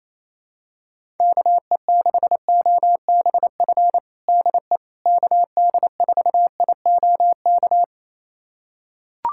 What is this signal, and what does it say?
Morse code, heard as a keyed tone, about 28 wpm, 700 Hz. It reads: KE6OBF DE KD4IOK
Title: USA Callsign DE Callsign 28wpm